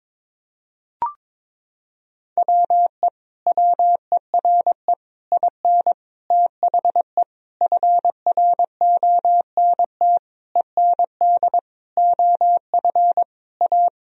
Morse code, heard as a keyed tone, about 22 wpm, 700 Hz. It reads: WE WERE IN THE FRONT END OF A
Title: Sentences from Top 1000 Words - Review Part 2 - 22wpm